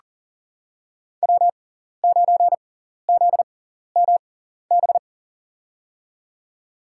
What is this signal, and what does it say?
Morse code, heard as a keyed tone, about 40 wpm, spaced as for 14 wpm, 700 Hz. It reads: W9ZMB
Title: USA Call Signs - 5 Characters - 6x Character Spacing - 40wpm